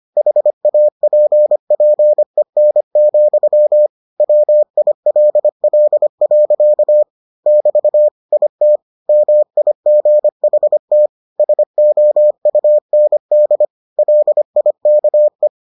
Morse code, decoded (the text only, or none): HAPPEN, WILL. <BT> IT MIGHT SOUND LIKE